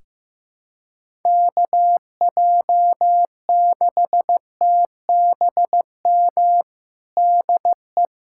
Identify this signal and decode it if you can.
Morse code, heard as a keyed tone, about 15 wpm, 700 Hz. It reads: KJ6TBM DE
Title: USA Callsign DE Callsign 15wpm